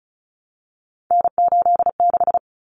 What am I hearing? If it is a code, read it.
Morse code, heard as a keyed tone, about 34 wpm, 700 Hz. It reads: N86
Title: Common Cut Numbers - 3-Digit 35wpm